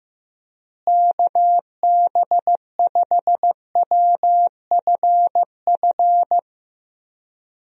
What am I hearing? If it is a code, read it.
Morse code, heard as a keyed tone, about 16 wpm, 700 Hz. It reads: KB5WFF
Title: USA Call Signs - 6 Characters - Repeated 3x - 15wpm